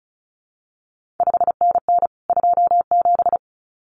Morse code, heard as a keyed tone, about 36 wpm, 700 Hz. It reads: 5NN 27
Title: CQ WW DX Contest Exchange Encore - 35wpm